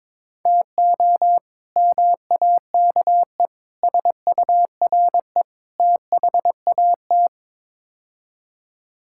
Morse code, heard as a keyed tone, about 22 wpm, 700 Hz. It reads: TO MAKE SURE THAT
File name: practice / 22wpm / Build A Phrase - 5-Word Phrases - 22wpm